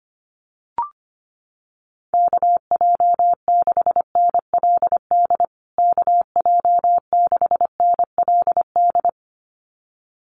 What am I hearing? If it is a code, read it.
Morse code, heard as a keyed tone, about 25 wpm, 700 Hz. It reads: KJ6NLD KJ6NLD
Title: USA Call Signs Repeated 2x 25wpm